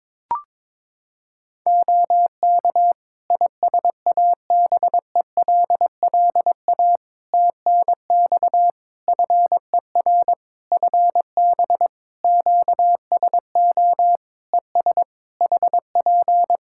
Morse code, heard as a keyed tone, about 22 wpm, 700 Hz. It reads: OK ISABELLA TNX FER FB QSO ES HP
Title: Practice QSO - Exchange 4 - Ending - 22wpm